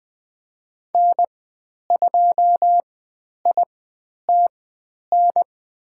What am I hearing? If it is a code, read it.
Morse code, heard as a keyed tone, about 20 wpm, spaced as for 10 wpm, 700 Hz. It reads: N2ITN
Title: 102 Lesson 17g - Letter u - Call Signs 20wpm